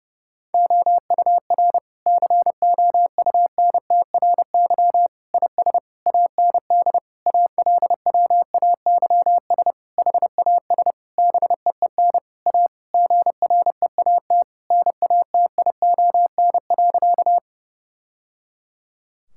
Morse code, heard as a keyed tone, about 30 wpm, 700 Hz. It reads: OUR COUNTRY IS AND ALWAYS HAS BEEN A GREAT NATION.
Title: Sentences from Top 1000 Words - Review Part 6 - 30wpm